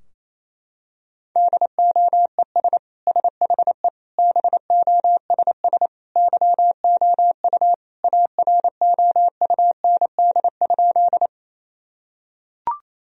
Morse code, heard as a keyed tone, about 28 wpm, 700 Hz. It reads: DOES SHE BOSS YOU AROUND?
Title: Sentences from Top 1000 Words - Review Part 4 - 28wpm